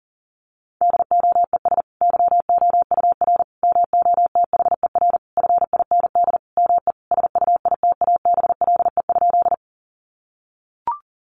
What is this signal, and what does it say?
Morse code, heard as a keyed tone, about 40 wpm, 700 Hz. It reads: DOES YOUR MOTHER FIND ME SUITABLE?
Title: Sentences from Top 1400 Words 40wpm